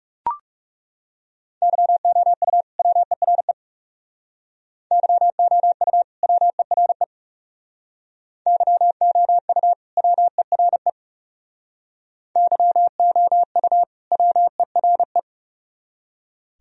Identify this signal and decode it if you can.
Morse code, heard as a keyed tone, about 35 wpm, 700 Hz. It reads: YOU WERE YOU WERE YOU WERE YOU WERE
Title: Speed-Racing - Sets of 2 Words 30wpm